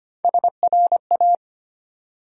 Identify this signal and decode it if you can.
Morse code, heard as a keyed tone, about 25 wpm, 700 Hz. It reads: SRA
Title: USA Call Sign Postfixes - 3 Letters 25wpm